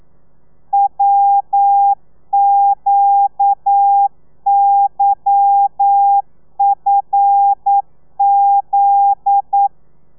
Morse code, heard as a keyed tone, about 9 wpm, 800 Hz. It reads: WQYFZ
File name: telegrafia / i2tfj / Lez11 40